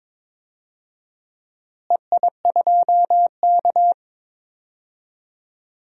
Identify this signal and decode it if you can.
Morse code, heard as a keyed tone, about 22 wpm, 700 Hz. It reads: EI2K
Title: DXCC-100 Call Signs - 4 Characters - Repeated 3x - 22wpm